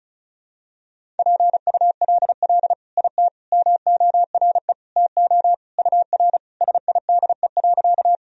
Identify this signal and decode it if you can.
Morse code, heard as a keyed tone, about 36 wpm, 700 Hz. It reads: PULL IT MORE TO UR SIDE.
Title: Sentences from Top 400 Words and Abbreviations 35wpm